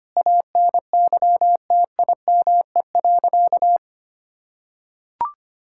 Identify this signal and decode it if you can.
Morse code, heard as a keyed tone, about 25 wpm, 700 Hz. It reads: ANYTIME.
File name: practice / 25wpm / Difficult Sentences with Hints - 25wpm